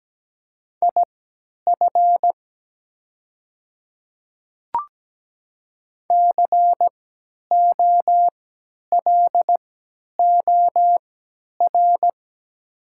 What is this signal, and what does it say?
Morse code, heard as a keyed tone, about 17 wpm, spaced as for 10 wpm, 700 Hz. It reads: IF COLOR
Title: 3x Character Spacing - Top 500 Words - 17wpm